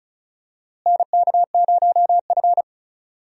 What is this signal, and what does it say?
Morse code, heard as a keyed tone, about 36 wpm, 700 Hz. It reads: NK0F
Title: USA Call Signs - 4 Characters - Repeated 3x - 35wpm